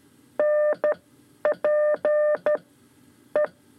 Morse code, heard as a keyed tone, about 12 wpm, spaced as for 9 wpm, 600 Hz. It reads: NP E
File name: 5 Lezione CW 2